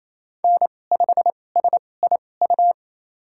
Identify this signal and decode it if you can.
Morse code, heard as a keyed tone, about 28 wpm, spaced as for 20 wpm, 700 Hz. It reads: N5SIU
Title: USA Call Signs - 5 Characters - 2x Character Spacing - 28wpm